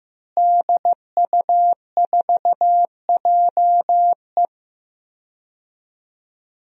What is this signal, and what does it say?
Morse code, heard as a keyed tone, about 16 wpm, 700 Hz. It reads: DU4JE
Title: DXCC-100 Call Signs - 5 Characters - Repeated 3x - 15wpm